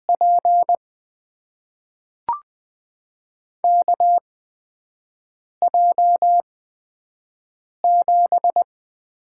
Morse code, heard as a keyed tone, about 21 wpm, spaced as for 5 wpm, 700 Hz. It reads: P KJ7
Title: USA Call Signs - 6 Characters - 8x Character Spacing - 20wpm